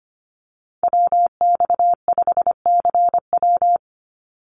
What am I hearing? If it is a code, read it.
Morse code, heard as a keyed tone, about 25 wpm, 700 Hz. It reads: WX5CW
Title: Call Signs Repeated 3x - 25wpm